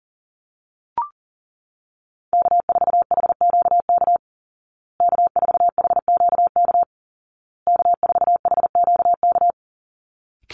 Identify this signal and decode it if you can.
Morse code, heard as a keyed tone, about 40 wpm, 700 Hz. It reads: K4HQK K4HQK K4HQK
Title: Call Signs Repeated 2x - 40wpm